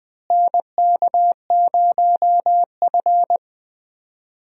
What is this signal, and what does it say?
Morse code, heard as a keyed tone, about 21 wpm, 700 Hz. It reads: NK0F
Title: USA Call Signs - 4 Characters - Repeated 3x - 20wpm